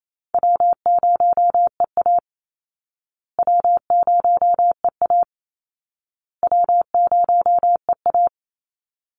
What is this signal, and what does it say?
Morse code, heard as a keyed tone, about 28 wpm, 700 Hz. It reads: W0EA W0EA W0EA
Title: Call Signs Repeated 2x - 28wpm